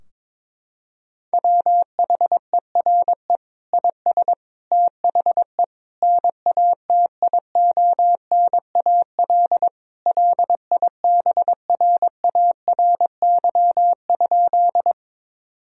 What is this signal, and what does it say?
Morse code, heard as a keyed tone, about 22 wpm, 700 Hz. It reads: WHERE IS THE NATIONAL LIBRARY?
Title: Sentences from Top 1000 Words - Review Part 4 - 22wpm